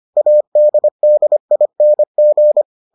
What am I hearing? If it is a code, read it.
Morse code, heard as a keyed tone, about 25 wpm, 600 Hz. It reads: ADDING